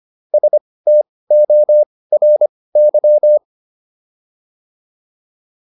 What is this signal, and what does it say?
Morse code, heard as a keyed tone, about 25 wpm, spaced as for 18 wpm, 600 Hz. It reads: STORY